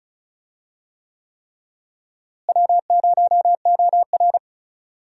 Morse code, heard as a keyed tone, about 36 wpm, 700 Hz. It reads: W0OR
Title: USA Call Signs - 4 Characters - Repeated 3x - 35wpm